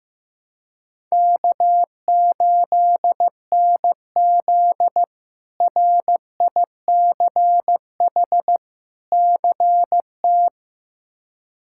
Morse code, heard as a keyed tone, about 15 wpm, 700 Hz. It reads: K8NZ RICH CT
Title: K1USN SST QSO Practice - 15wpm